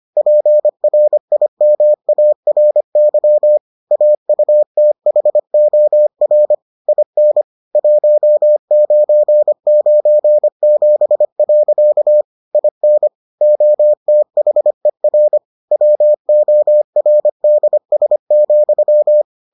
Morse code, heard as a keyed tone, about 25 wpm, 600 Hz. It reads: PRIMARY AUTHOR IN 1997. IN OTHER WORDS,